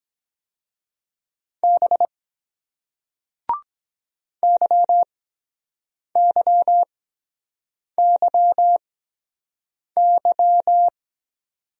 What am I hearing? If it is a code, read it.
Morse code, heard as a keyed tone, about 23 wpm, spaced as for 7 wpm, 700 Hz. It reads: B YYYY